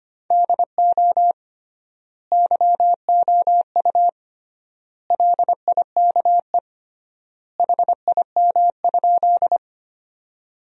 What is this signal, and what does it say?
Morse code, heard as a keyed tone, about 25 wpm, 700 Hz. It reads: DO YOU LIKE HIM?
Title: 3x Word-Spacing - Sentences from Top 100 Words - 25wpm